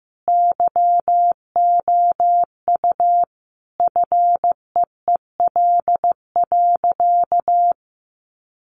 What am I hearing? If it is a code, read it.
Morse code, heard as a keyed tone, about 15 wpm, 700 Hz. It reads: YOU FEEL.
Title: Sentences from Top 1000 Words 15wpm